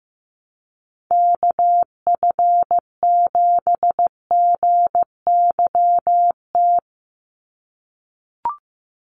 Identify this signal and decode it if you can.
Morse code, heard as a keyed tone, about 15 wpm, 700 Hz. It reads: KF7GYT
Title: USA Call Signs 15wpm